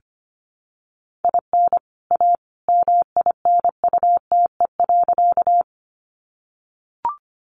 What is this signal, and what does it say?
Morse code, heard as a keyed tone, about 24 wpm, 700 Hz. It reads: IN A MINUTE.
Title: Sentences from Top 500 Words 25wpm